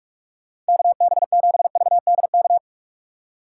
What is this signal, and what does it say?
Morse code, heard as a keyed tone, about 57 wpm, 700 Hz. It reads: KB7VDK